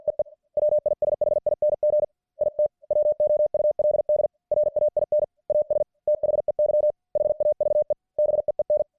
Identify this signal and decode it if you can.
Morse code, heard as a keyed tone, about 50 wpm, 600 Hz. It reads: EE WISHING IT WOULD RAIN AS THEY HAVE BEEN